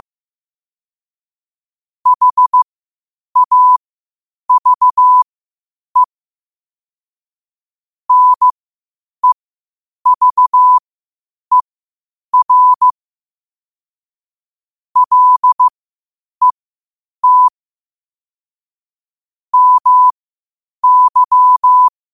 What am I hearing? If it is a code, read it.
Morse code, heard as a keyed tone, about 15 wpm, spaced as for 8 wpm, 1000 Hz. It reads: HAVE NEVER LET MY
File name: qotdfiles / 8wpm